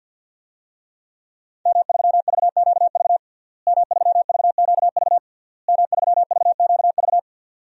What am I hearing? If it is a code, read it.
Morse code, heard as a keyed tone, about 52 wpm, 700 Hz. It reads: M3VQV M3VQV M3VQV
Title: DXCC-100 Call Signs - 5 Characters - Repeated 3x - 50wpm